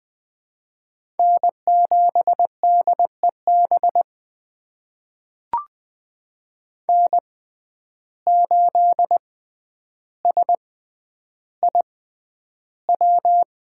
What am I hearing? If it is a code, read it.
Morse code, heard as a keyed tone, about 20 wpm, spaced as for 10 wpm, 700 Hz. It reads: N7DEB N8SIW
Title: USA Call Signs - 5 Characters - 6x Character Spacing - 20wpm